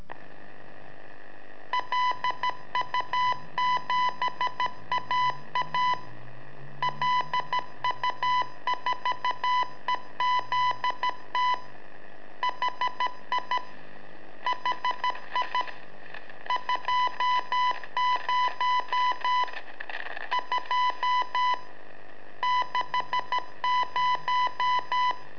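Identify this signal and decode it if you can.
Morse code, heard as a keyed tone, about 13 wpm, 1000 Hz. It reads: LU7AA LU4EZT HI HI 20 2 60
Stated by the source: Telemetry in Morse from LU7AA and LU4EZ, AMSAT and Turma Mar del Plata, sent from a balloon.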